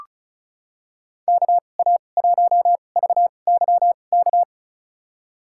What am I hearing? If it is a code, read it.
Morse code, heard as a keyed tone, about 34 wpm, spaced as for 25 wpm, 700 Hz. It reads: KA1VYK
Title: USA Call Signs - 2x Character Spacing - 35wpm